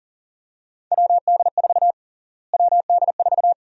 Morse code, heard as a keyed tone, about 40 wpm, 700 Hz. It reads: WD4 WD4
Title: USA Call Sign Prefixes - Repeated 2x - 40wpm